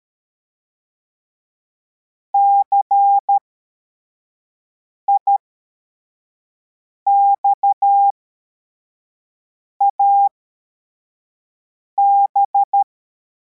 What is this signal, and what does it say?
Morse code, heard as a keyed tone, about 13 wpm, spaced as for 4 wpm, 800 Hz. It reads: CIXAB
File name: formation / cw / Text62